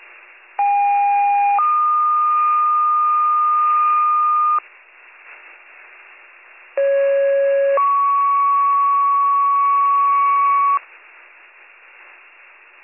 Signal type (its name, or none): two-tone page